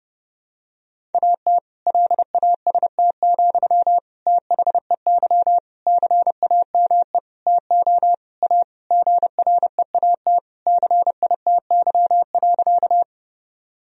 Morse code, heard as a keyed tone, about 32 wpm, 700 Hz. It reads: AT LAST, THEY CAME TO A GREAT CITY.
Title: Sentences from Top 1000 Words - Review Part 1 - 30wpm